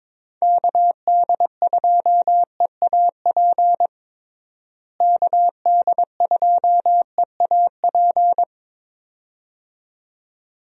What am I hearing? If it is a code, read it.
Morse code, heard as a keyed tone, about 22 wpm, 700 Hz. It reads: KD2EAP KD2EAP
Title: USA Call Signs - 6 Characters - Repeated 2x - 22wpm